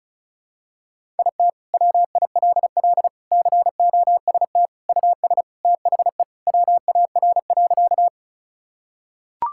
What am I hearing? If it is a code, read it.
Morse code, heard as a keyed tone, about 36 wpm, 700 Hz. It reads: IT WILL COST US THE WAR.
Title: Sentences from Top 1000 Words - Review Part 5 - 35wpm